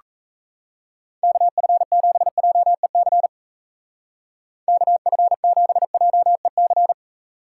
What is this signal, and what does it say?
Morse code, heard as a keyed tone, about 34 wpm, 700 Hz. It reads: KF7JEC KF7JEC
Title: Speed Racing - USA 2x3 Call Signs 28wpm